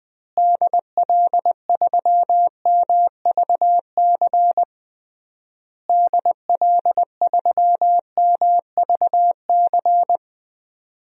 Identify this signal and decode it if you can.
Morse code, heard as a keyed tone, about 20 wpm, 700 Hz. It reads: DL3MVC DL3MVC
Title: DXCC-100 Call Signs - 6 Characters - Repeated 3x - 20wpm